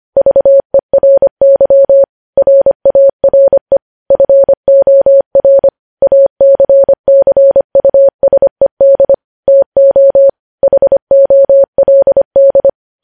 Morse code, heard as a keyed tone, about 25 wpm, 550 Hz. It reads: VERY RARE FOR ACCUSED TO HOLD
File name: podcast / News headlines-1x-25x25